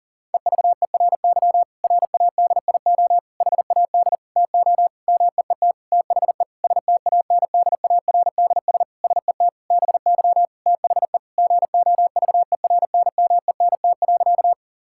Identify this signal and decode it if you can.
Morse code, heard as a keyed tone, about 37 wpm, 700 Hz. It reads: EVERY RADIO HAD TO MEET THE STANDARDS SET BY THE GOVERNMENT.
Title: Sentences from Top 1000 Words - Review Part 6 - 40wpm